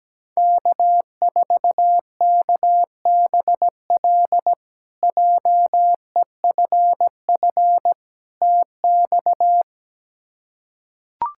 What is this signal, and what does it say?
Morse code, heard as a keyed tone, about 17 wpm, 700 Hz. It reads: K4KBL JEFF TX
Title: K1USN SST QSO Practice - 17wpm